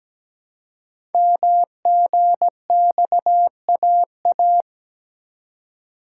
Morse code, heard as a keyed tone, about 17 wpm, 700 Hz. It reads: MGXAA
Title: Code Groups - 5 Letters - 17wpm